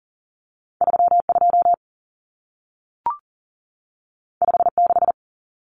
Morse code, heard as a keyed tone, about 40 wpm, 700 Hz. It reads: 32 56
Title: Numbers - 2-Digit 40wpm